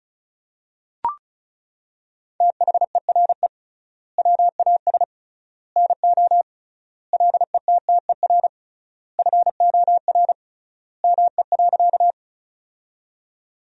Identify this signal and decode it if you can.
Morse code, heard as a keyed tone, about 34 wpm, 700 Hz. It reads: THERE WAS NO LETTER FOR ME.
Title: 3x Word-Spacing - Sentences from Top 300 Words - 35wpm